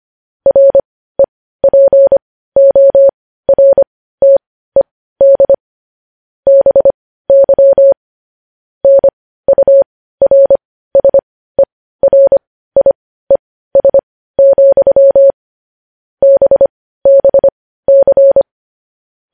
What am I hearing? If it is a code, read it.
Morse code, heard as a keyed tone, about 25 wpm, spaced as for 15 wpm, 550 Hz. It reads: REPORTED BY NURSERIES, BBC
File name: podcast / News headlines-1x-25x15